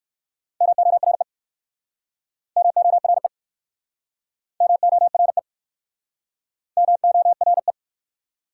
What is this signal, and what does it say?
Morse code, heard as a keyed tone, about 58 wpm, 700 Hz. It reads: MORE MORE MORE MORE